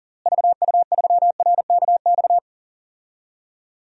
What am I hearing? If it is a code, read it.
Morse code, heard as a keyed tone, about 40 wpm, 700 Hz. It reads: VU3RKX
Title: DXCC-100 Call Signs - 6 Characters - Repeated 3x - 40wpm